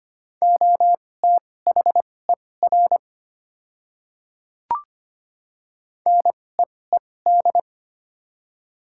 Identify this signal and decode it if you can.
Morse code, heard as a keyed tone, about 25 wpm, spaced as for 18 wpm, 700 Hz. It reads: OTHER NEED
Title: 2x Character Spacing - Top 200 Words - 25wpm